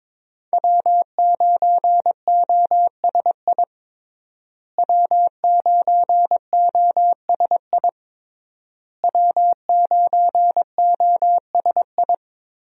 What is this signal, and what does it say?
Morse code, heard as a keyed tone, about 22 wpm, 700 Hz. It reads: W9OSI W9OSI W9OSI
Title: USA Call Signs - 5 Characters - Repeated 3x - 22wpm